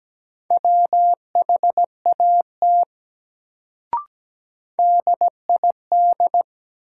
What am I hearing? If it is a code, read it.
Morse code, heard as a keyed tone, about 17 wpm, 700 Hz. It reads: WHAT DID
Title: Sentences from Top 100 Words - New Word Drill - 17wpm